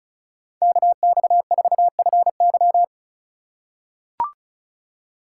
Morse code, heard as a keyed tone, about 36 wpm, 700 Hz. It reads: KX4FY
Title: USA Call Signs - 5 Characters - 8x Character Spacing - 35wpm